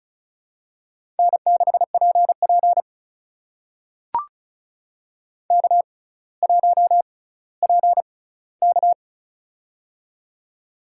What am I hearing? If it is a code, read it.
Morse code, heard as a keyed tone, about 36 wpm, spaced as for 18 wpm, 700 Hz. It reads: N6PP K1PK
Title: USA Call Signs - 4 Characters - 6x Character Spacing - 35wpm